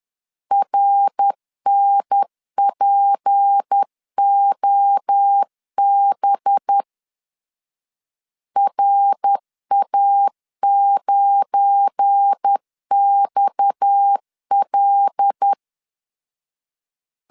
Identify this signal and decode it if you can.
Morse code, heard as a keyed tone, about 10 wpm, 800 Hz. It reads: RNPOB RA9XL
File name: LEZIONI / 7 GF9Z / 7 GF9 10wpm 2747